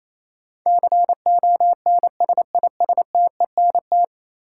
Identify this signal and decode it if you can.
Morse code, heard as a keyed tone, about 28 wpm, 700 Hz. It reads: CONSISTENT
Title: Sentences from Top 1500 Words - New Word Drill Alt - 28wpm